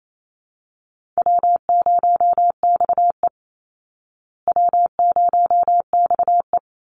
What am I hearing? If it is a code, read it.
Morse code, heard as a keyed tone, about 28 wpm, 700 Hz. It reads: W0XE W0XE
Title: Easy Call Signs Repeated 2x - 28wpm